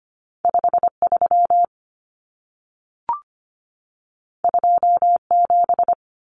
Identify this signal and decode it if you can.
Morse code, heard as a keyed tone, about 25 wpm, 700 Hz. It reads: 53 27
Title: Common Cut Numbers - 2-Digit 25wpm